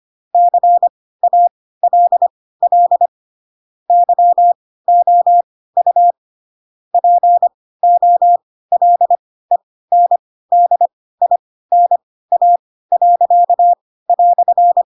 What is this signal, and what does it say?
Morse code, heard as a keyed tone, about 25 wpm, spaced as for 16 wpm, 700 Hz. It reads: CALL YOU POLENDINA."
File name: Pinocchio / Pinocchio ch2 10 25 ik1hge